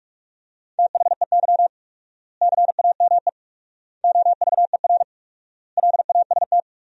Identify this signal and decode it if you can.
Morse code, heard as a keyed tone, about 46 wpm, 700 Hz. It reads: THEY CAME OVER LAST
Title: 4x Word-Spacing - Sentences from Top 300 Words - 45wpm